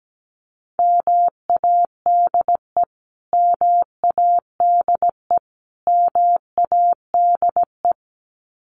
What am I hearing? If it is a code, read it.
Morse code, heard as a keyed tone, about 17 wpm, 700 Hz. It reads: MADE MADE MADE
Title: Top 200 Words Repeated 17wpm